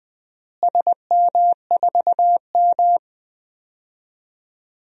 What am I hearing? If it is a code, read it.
Morse code, heard as a keyed tone, about 19 wpm, 700 Hz. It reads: SM4M
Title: DXCC-100 Call Signs - 4 Characters - Repeated 3x - 20wpm